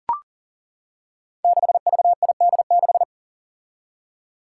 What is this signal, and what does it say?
Morse code, heard as a keyed tone, about 41 wpm, 700 Hz. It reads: 6VID6
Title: Five Challenging Letter-Number Combinations 40wpm